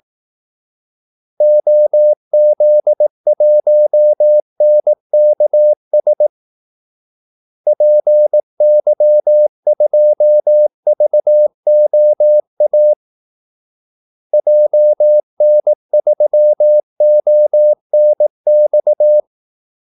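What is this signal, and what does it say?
Morse code, heard as a keyed tone, about 18 wpm, 600 Hz. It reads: OZ1NKS PY2VOA JN3ONX
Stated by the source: Morse call-sign practice at 18 wpm.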